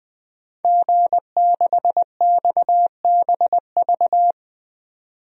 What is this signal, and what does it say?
Morse code, heard as a keyed tone, about 20 wpm, 700 Hz. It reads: G6XBV
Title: DXCC-100 Call Signs - 5 Characters - 20wpm